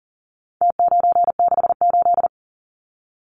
Morse code, heard as a keyed tone, about 40 wpm, 700 Hz. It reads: T968
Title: Numbers - Contest Style - 4-Digit 40wpm